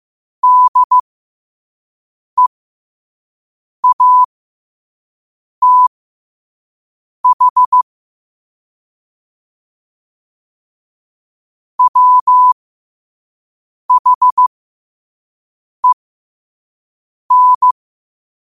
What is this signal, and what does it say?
Morse code, heard as a keyed tone, about 15 wpm, spaced as for 5 wpm, 1000 Hz. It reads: DEATH WHEN
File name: qotdfiles / 5wpm